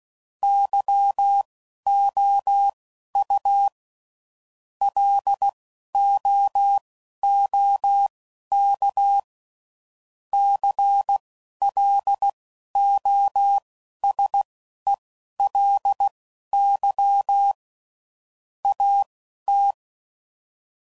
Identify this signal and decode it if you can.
Morse code, heard as a keyed tone, about 16 wpm, spaced as for 11 wpm, 800 Hz. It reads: YOU LOOK CLOSELY AT
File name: CW Code Groups @10WPM / Technical Subjects / OR Gate